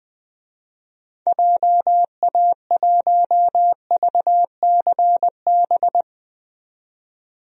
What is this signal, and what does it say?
Morse code, heard as a keyed tone, about 20 wpm, 700 Hz. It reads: JA1VCB
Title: DXCC-100 Call Signs - 6 Characters - 20wpm